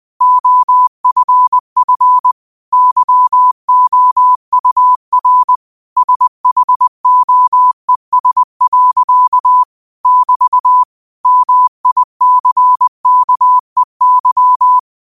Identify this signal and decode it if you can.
Morse code, heard as a keyed tone, about 20 wpm, 1000 Hz. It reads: OFF YOUR SHOES. <BT> MICKEY
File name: qotdfiles / 20wpm